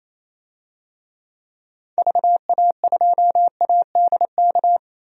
Morse code, heard as a keyed tone, about 28 wpm, 700 Hz. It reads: VA2ADK
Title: DXCC-100 Call Signs - 6 Characters - 28wpm